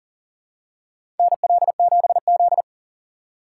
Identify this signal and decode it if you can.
Morse code, heard as a keyed tone, about 40 wpm, 700 Hz. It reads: NL7Z
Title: USA Call Signs - 6x Character Spacing - 40wpm